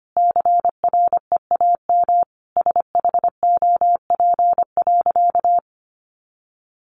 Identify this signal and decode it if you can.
Morse code, heard as a keyed tone, about 25 wpm, 700 Hz. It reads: CREAM SHOP.
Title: Easy Phrases 25wpm